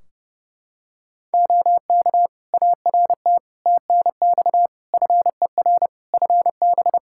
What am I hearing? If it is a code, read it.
Morse code, heard as a keyed tone, about 30 wpm, 700 Hz. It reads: OK ART TNX FER FB
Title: Practice QSO - Exchange 4 - Ending - 30wpm